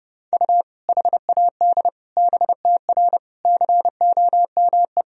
Morse code, heard as a keyed tone, about 31 wpm, 700 Hz. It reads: U HAD BTR COME
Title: Sentences from Top 400 Words and Abbreviations 30wpm